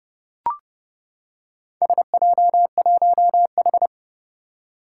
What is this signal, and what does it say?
Morse code, heard as a keyed tone, about 30 wpm, 700 Hz. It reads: SJ1H
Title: Four Challenging Letter-Number Combinations 30wpm